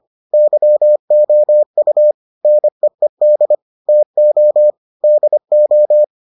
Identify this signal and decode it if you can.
Morse code, heard as a keyed tone, about 25 wpm, 600 Hz. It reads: YOU NEED TO DO